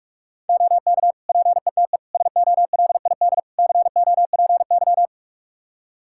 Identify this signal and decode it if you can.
Morse code, heard as a keyed tone, about 46 wpm, 700 Hz. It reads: OK PETE SOLID COPY
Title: Practice QSO - Exchange 3 - More Info - 45wpm